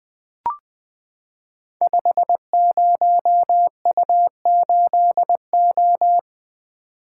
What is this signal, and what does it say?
Morse code, heard as a keyed tone, about 20 wpm, 700 Hz. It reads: 50U8O
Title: Code Groups - 5 Letter:Numbers - 20wpm